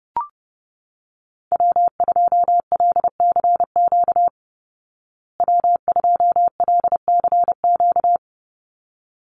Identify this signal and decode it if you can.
Morse code, heard as a keyed tone, about 30 wpm, 700 Hz. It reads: W2LCQ W2LCQ
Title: Call Signs Repeated 3x - 30wpm